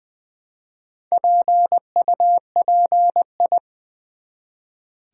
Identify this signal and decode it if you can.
Morse code, heard as a keyed tone, about 20 wpm, 700 Hz. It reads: PUPI
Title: Build A Word - 1 Letter At A Time - 5-Letter Words - 20wpm